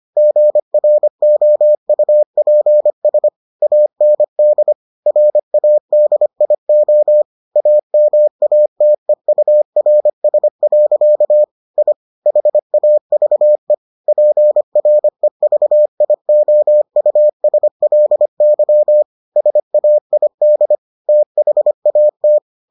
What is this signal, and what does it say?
Morse code, heard as a keyed tone, about 25 wpm, 600 Hz. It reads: GROUPS AND RADIO AMATEURS. I HAVE PREVIOUSLY SAID THAT